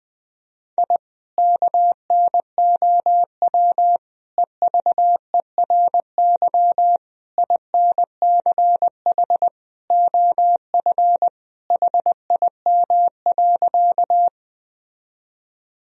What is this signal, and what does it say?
Morse code, heard as a keyed tone, about 21 wpm, 700 Hz. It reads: I KNOW EVERY INCH OF HIM.
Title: Sentences from Top 1000 Words - Review Part 3 - 20wpm